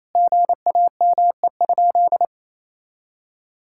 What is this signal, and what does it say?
Morse code, heard as a keyed tone, about 27 wpm, 700 Hz. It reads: GAME?
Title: Sentences from Top 1600 Words 28wpm